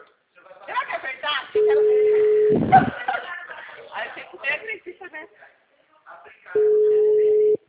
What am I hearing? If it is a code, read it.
Ringback tone (Europe, 425 Hz), with speech.
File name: out- -2048- - - .45094